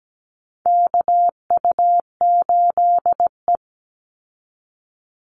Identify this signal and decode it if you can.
Morse code, heard as a keyed tone, about 17 wpm, 700 Hz. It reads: KU8E
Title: Call Signs Repeated 3x - 17wpm